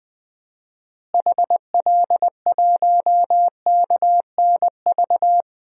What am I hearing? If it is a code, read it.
Morse code, heard as a keyed tone, about 20 wpm, 700 Hz. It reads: HL1KNV
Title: DXCC-100 Call Signs - 6 Characters - Repeated 3x - 20wpm